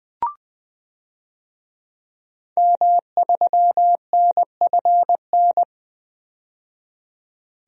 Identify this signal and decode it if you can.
Morse code, heard as a keyed tone, about 21 wpm, 700 Hz. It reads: M3NFN
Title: DXCC-100 Call Signs - 5 Characters - 20wpm